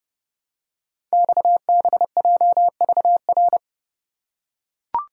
Morse code, heard as a keyed tone, about 29 wpm, 700 Hz. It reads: XBJVR
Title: Code Groups - 5 Letters - 30wpm